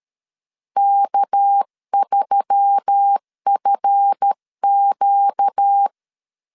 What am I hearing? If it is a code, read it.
Morse code, heard as a keyed tone, about 13 wpm, 800 Hz. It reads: K3FQ
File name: LEZIONI / 7 GF9Z / 7 GF 12wpm 2122